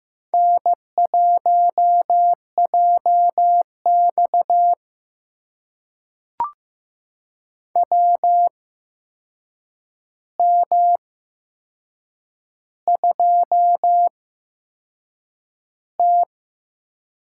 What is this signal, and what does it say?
Morse code, heard as a keyed tone, about 15 wpm, spaced as for 6 wpm, 700 Hz. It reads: N1JX WM2T
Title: USA Call Signs - 4 Characters - 8x Character Spacing - 15wpm